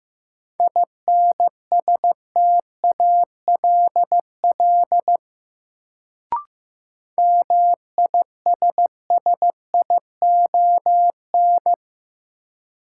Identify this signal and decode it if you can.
Morse code, heard as a keyed tone, about 15 wpm, 700 Hz. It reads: INSTALL MISSION
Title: Sentences from Top 1600 Words - New Word Drill Alt - 15wpm